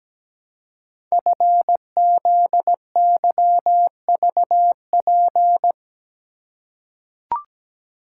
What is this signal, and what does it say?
Morse code, heard as a keyed tone, about 17 wpm, 690 Hz. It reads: FZYVP
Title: Code Groups - 5 Letter:Numbers - 17wpm